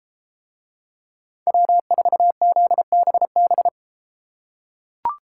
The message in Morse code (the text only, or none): W4ZBB